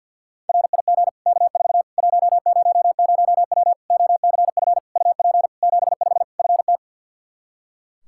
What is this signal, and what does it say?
Morse code, heard as a keyed tone, about 48 wpm, 700 Hz. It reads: RIG K4 100W OCF UP 75 FT
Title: Practice QSO - Exchange 2 - Info - 50wpm